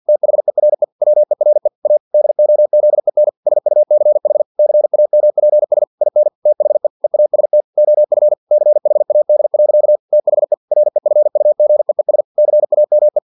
Morse code, heard as a keyed tone, about 50 wpm, 600 Hz. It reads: THERE WERE A DOZEN SUCH CAMPS IN THE EAST OF CHAD. THE REFUGEES CAME
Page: Never 49wpm